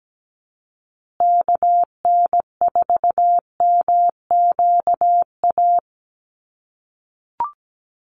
Morse code, heard as a keyed tone, about 17 wpm, 700 Hz. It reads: KN4MQA
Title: USA Call Signs Repeated 3x 17wpm